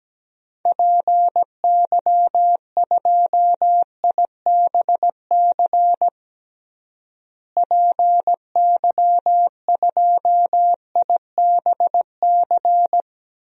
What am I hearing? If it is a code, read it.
Morse code, heard as a keyed tone, about 17 wpm, 700 Hz. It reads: PY2IBC PY2IBC
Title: DXCC-100 Call Signs - 6 Characters - Repeated 3x - 17wpm